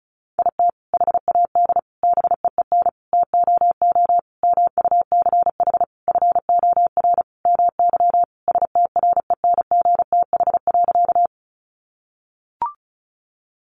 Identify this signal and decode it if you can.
Morse code, heard as a keyed tone, about 34 wpm, 700 Hz. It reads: IT HAD BEEN TOO MUCH FOR MY STRENGTH.
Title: Sentences from Top 1300 Words 35wpm